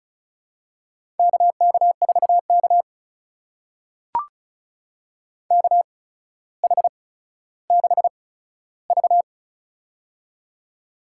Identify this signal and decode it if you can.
Morse code, heard as a keyed tone, about 36 wpm, spaced as for 15 wpm, 700 Hz. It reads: KK4K KH6V
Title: USA Call Signs - 4 Characters - 8x Character Spacing - 35wpm